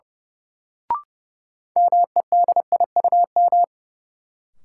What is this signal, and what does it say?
Morse code, heard as a keyed tone, about 31 wpm, 700 Hz. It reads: MEDIUM
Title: Sentences from Top 900 Words - New Word Drill - 30wpm